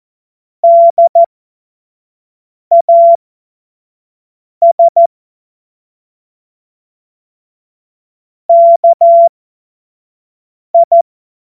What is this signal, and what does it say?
Morse code, heard as a keyed tone, about 14 wpm, spaced as for 5 wpm, 680 Hz. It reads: DAS KI
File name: Morsekurs / Witze / Witz 21. September 2021, 25 F70